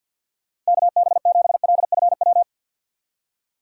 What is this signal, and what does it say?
Morse code, heard as a keyed tone, about 63 wpm, 700 Hz. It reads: KB7LFW